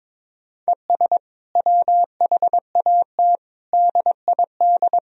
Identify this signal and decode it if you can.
Morse code, heard as a keyed tone, about 22 wpm, 700 Hz. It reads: ES WHAT DID
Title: Sentences from Top 200 Words and Abbreviations 22wpm